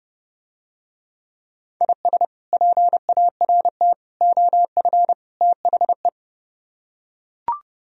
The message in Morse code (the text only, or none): IS PART OF THE